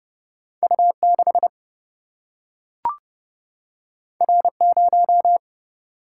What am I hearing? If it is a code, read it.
Morse code, heard as a keyed tone, about 30 wpm, 700 Hz. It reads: U 6 R 0
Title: Single-Letter Single-Number 30wpm